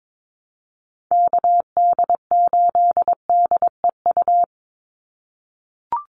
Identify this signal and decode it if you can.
Morse code, heard as a keyed tone, about 22 wpm, 700 Hz. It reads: KD8DEU
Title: NA SOTA Chaser Callsigns 22wpm